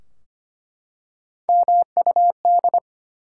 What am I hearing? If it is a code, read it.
Morse code, heard as a keyed tone, about 25 wpm, 700 Hz. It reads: MUD
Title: Three Letter Words - 25wpm